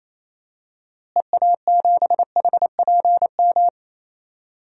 Morse code, heard as a keyed tone, about 28 wpm, 700 Hz. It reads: EA7HPM
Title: DXCC-100 Call Signs - 6 Characters - Repeated 3x - 28wpm